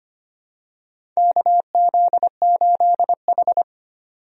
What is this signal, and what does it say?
Morse code, heard as a keyed tone, about 25 wpm, 700 Hz. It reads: KZ8H
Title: USA Call Signs - 4 Characters - 25wpm